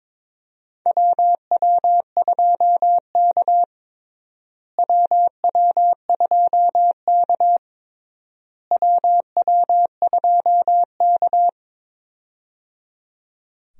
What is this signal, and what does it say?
Morse code, heard as a keyed tone, about 22 wpm, 700 Hz. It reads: WW2K WW2K WW2K
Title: USA Call Signs - 4 Characters - Repeated 3x - 22wpm